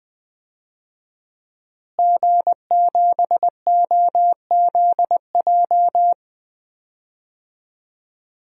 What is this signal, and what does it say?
Morse code, heard as a keyed tone, about 20 wpm, 700 Hz. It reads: G7OZJ
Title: DXCC-100 Call Signs - 5 Characters - Repeated 3x - 20wpm